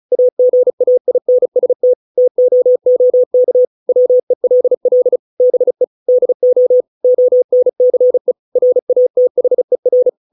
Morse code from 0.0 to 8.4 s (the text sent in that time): AGAINST TOOK WELL BE DO ONCE